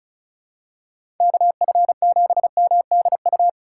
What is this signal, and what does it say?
Morse code, heard as a keyed tone, about 34 wpm, 700 Hz. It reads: KF7MDU
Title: USA Call Signs - 6 Characters - 3x Character Spacing - 35wpm